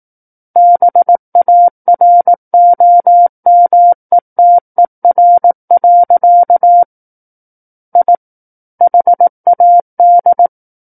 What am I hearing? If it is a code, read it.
Morse code, heard as a keyed tone, about 18 wpm, 700 Hz. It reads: BAROMETER. I HAD
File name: 18 WPM